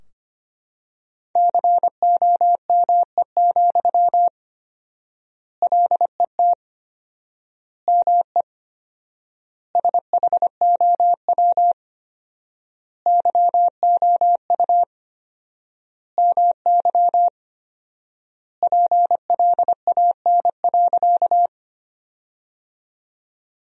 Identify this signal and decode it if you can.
Morse code, heard as a keyed tone, about 25 wpm, 700 Hz. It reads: COME, LET ME SHOW YOU MY PLAN.
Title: 4x Word-Spacing - Sentences from Top 500 Words - 25wpm